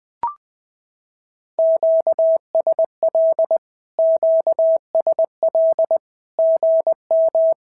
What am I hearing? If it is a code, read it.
Morse code, heard as a keyed tone, about 20 wpm, 650 Hz. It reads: QSL QSL GM
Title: SOTA QSO Practice 20wpm